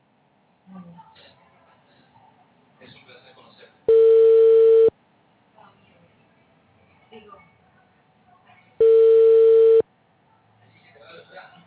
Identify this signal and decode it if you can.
Ringback tone (China, 450 Hz), with speech.